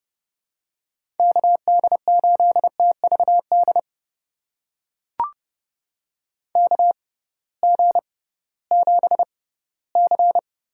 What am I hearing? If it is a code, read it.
Morse code, heard as a keyed tone, about 30 wpm, spaced as for 18 wpm, 700 Hz. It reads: KD8TVD KG7C
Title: USA Call Signs - 6 Characters - 6x Character Spacing - 30wpm